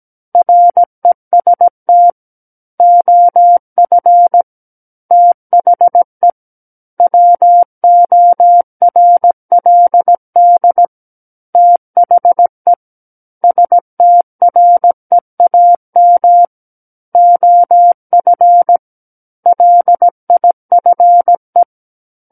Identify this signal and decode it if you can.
Morse code, heard as a keyed tone, about 17 wpm, 700 Hz. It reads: REST OF THE WORLD THE STREAM OF LIFE
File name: Book One / 08-Chapter 8 - 17 WPM